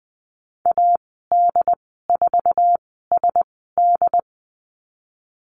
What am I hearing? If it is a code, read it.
Morse code, heard as a keyed tone, about 20 wpm, 700 Hz. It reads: A D 4 S D
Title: 076 Lesson 13g - Letter L - Call Signs 20wpm